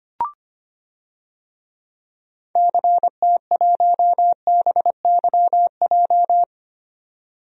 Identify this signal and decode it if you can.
Morse code, heard as a keyed tone, about 25 wpm, 700 Hz. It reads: CT1BYJ
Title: DXCC-100 Call Signs - 6 Characters - 25wpm